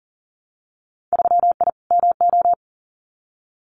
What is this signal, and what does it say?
Morse code, heard as a keyed tone, about 39 wpm, 700 Hz. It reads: 3I MO
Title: Winter Field Day Exchanges 40wpm